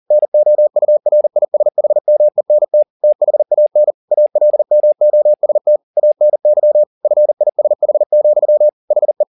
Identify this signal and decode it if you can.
Morse code, heard as a keyed tone, about 40 wpm, 600 Hz. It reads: NOURISHMENT THAN ALMOST ANY FISH, HE
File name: the old man and the sea 2 40wpm